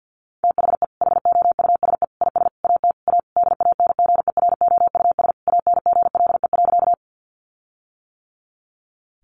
Morse code, heard as a keyed tone, about 49 wpm, 700 Hz. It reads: THE HOUSE IS AT A DANGEROUS ANGLE.
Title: Sentences from Top 1500 Words 50wpm